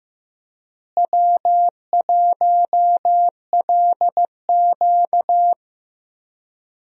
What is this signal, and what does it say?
Morse code, heard as a keyed tone, about 15 wpm, 700 Hz. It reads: W1LQ
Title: USA Call Signs - 4 Characters - Repeated 3x - 15wpm